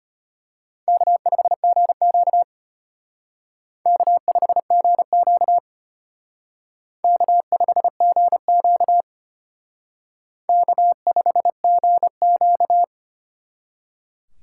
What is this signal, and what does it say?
Morse code, heard as a keyed tone, about 32 wpm, 700 Hz. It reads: K5GQ K5GQ K5GQ K5GQ